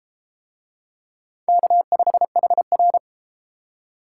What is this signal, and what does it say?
Morse code, heard as a keyed tone, about 31 wpm, 700 Hz. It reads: K5HR